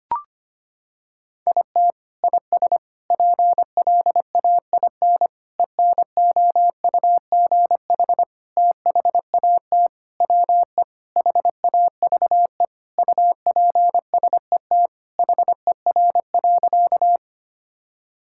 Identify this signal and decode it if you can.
Morse code, heard as a keyed tone, about 25 wpm, 700 Hz. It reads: IT IS PLAIN ENOUGH THAT WE HAVE UPSET HER.
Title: Sentences from Top 1000 Words - Review Part 5 - 25wpm